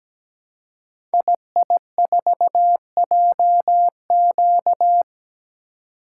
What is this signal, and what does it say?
Morse code, heard as a keyed tone, about 17 wpm, 700 Hz. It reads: II4JQ
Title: Code Groups - 5 Letter:Numbers - 17wpm